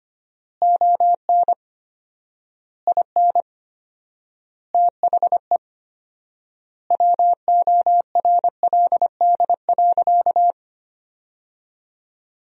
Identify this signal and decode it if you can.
Morse code, heard as a keyed tone, about 25 wpm, 700 Hz. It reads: ON IN THE WORLD.
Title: 4x Word-Spacing - Sentences from Top 200 Words - 25wpm